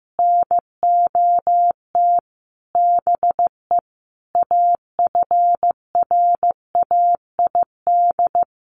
Morse code, heard as a keyed tone, about 15 wpm, 700 Hz. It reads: NOT BE AFRAID
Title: Sentences from Top 1500 Words 15wpm